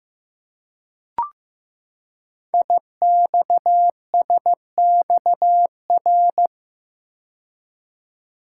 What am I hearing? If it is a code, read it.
Morse code, heard as a keyed tone, about 15 wpm, 700 Hz. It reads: IXSXR
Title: Code Groups - 5 Letter:Numbers - 15wpm